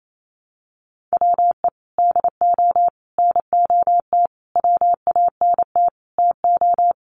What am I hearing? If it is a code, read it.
Morse code, heard as a keyed tone, about 28 wpm, 700 Hz. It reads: WE DO NOT WANT TO
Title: Sets of 5 Words 28wpm